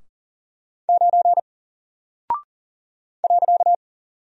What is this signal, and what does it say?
Morse code, heard as a keyed tone, about 40 wpm, 700 Hz. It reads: 9 .
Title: Single Character (Letter:Number:Punctuation:Prosign) Alt 40wpm